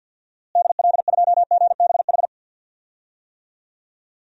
Morse code, heard as a keyed tone, about 52 wpm, 700 Hz. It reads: DL2GBH
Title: DXCC-100 Call Signs - 6 Characters - Repeated 3x - 50wpm